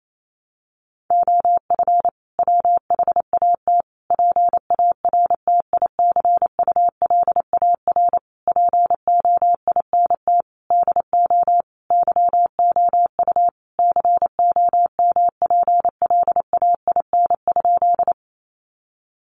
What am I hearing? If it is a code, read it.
Morse code, heard as a keyed tone, about 28 wpm, 700 Hz. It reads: OF WHAT PARTICULAR POINT DO YOU COMPLAIN?
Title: Sentences from Top 1300 Words 28wpm